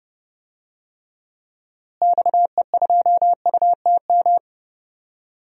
Morse code, heard as a keyed tone, about 30 wpm, 700 Hz. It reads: XE2UTM
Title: DXCC-100 Call Signs - 6 Characters - Repeated 2x - 30wpm